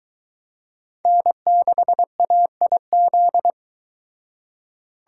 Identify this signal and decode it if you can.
Morse code, heard as a keyed tone, about 23 wpm, 700 Hz. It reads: N6AIZ